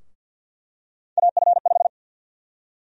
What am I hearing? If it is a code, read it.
Morse code, heard as a keyed tone, about 51 wpm, 700 Hz. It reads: AF5
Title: USA Call Sign Prefixes - Repeated 2x - 50wpm